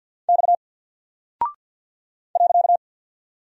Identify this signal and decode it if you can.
Morse code, heard as a keyed tone, about 48 wpm, 700 Hz. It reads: X .
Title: Single Character (Letter:Number:Punctuation:Prosign) Alt 50wpm